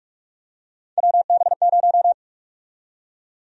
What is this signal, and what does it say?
Morse code, heard as a keyed tone, about 46 wpm, 700 Hz. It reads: WB0
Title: USA Call Sign Prefixes - Repeated 2x - 45wpm